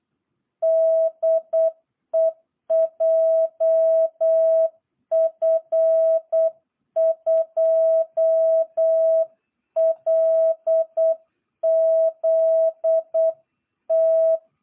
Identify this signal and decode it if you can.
Morse code, heard as a keyed tone, about 8 wpm, 660 Hz. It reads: DE JF2LZT
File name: jh8cbh / 2022cw